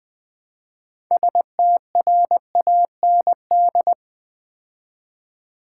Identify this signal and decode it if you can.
Morse code, heard as a keyed tone, about 20 wpm, 700 Hz. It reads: STRAND
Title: 6 Letter Words - First 3 Letters Given - 20wpm